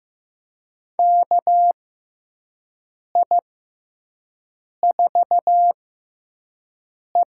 Morse code, heard as a keyed tone, about 15 wpm, spaced as for 5 wpm, 700 Hz. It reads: KI4E
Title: USA Call Signs - 6 Characters - 6x Character Spacing - 15wpm